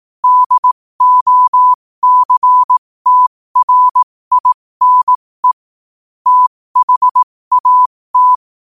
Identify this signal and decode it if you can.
Morse code, heard as a keyed tone, about 18 wpm, spaced as for 15 wpm, 1000 Hz. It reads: DOCTRINE THAT
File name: qotdfiles / 15wpm